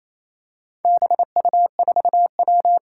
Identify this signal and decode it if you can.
Morse code, heard as a keyed tone, about 27 wpm, 700 Hz. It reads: BU4W
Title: Four Challenging Letter-Number Combinations 28wpm